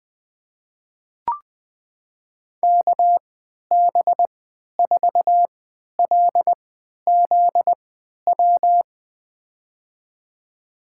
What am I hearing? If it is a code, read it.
Morse code, heard as a keyed tone, about 21 wpm, spaced as for 11 wpm, 700 Hz. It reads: KB4LZW
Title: USA Call Signs - 6 Characters - 3x Character Spacing - 20wpm